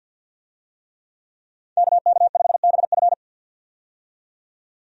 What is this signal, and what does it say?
Morse code, heard as a keyed tone, about 51 wpm, 700 Hz. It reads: KK5BF
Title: USA Call Signs - 5 Characters - 50wpm